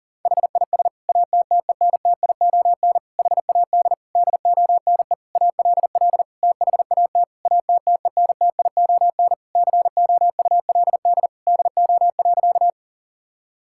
Morse code, heard as a keyed tone, about 40 wpm, 700 Hz. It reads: HIS ATTENTION HAD DONE ALL THAT ATTENTION COULD DO.
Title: Sentences from Top 1000 Words - Review Part 6 - 40wpm